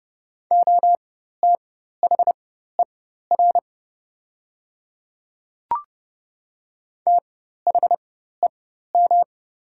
Morse code, heard as a keyed tone, about 30 wpm, spaced as for 14 wpm, 700 Hz. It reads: OTHER THEM
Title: 4x Character Spacing - Top 300 Words - 30wpm